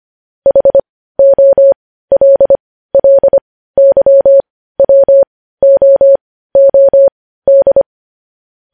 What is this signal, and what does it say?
Morse code, heard as a keyed tone, about 25 wpm, spaced as for 15 wpm, 550 Hz. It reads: HOLLYWOOD
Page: News headlines in Morse code at 15 wpm